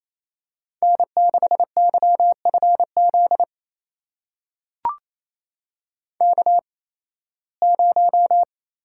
Morse code, heard as a keyed tone, about 28 wpm, spaced as for 18 wpm, 700 Hz. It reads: N6YFZ K0
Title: USA Call Signs - 5 Characters - 8x Character Spacing - 28wpm